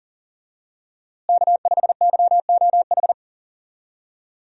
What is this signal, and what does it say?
Morse code, heard as a keyed tone, about 37 wpm, 700 Hz. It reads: K5YOH
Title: USA Call Signs - 5 Characters - Repeated 2x - 40wpm